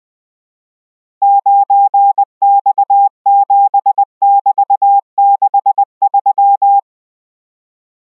Morse code, heard as a keyed tone, about 20 wpm, 800 Hz. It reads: 9X7<BT>63